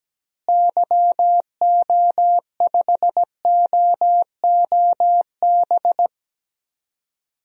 Morse code, heard as a keyed tone, about 17 wpm, 700 Hz. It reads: YO5OOB
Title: DXCC-100 Call Signs - 6 Characters - Repeated 3x - 17wpm